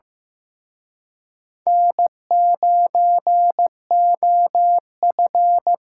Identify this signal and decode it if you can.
Morse code, heard as a keyed tone, about 18 wpm, 700 Hz. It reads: N9OF
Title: Speed Racing - USA Call Signs 15wpm